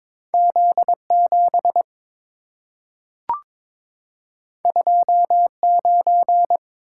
Morse code, heard as a keyed tone, about 23 wpm, 700 Hz. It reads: Z7 29
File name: practice / 22wpm / Two Letter-Number Combinations - 22wpm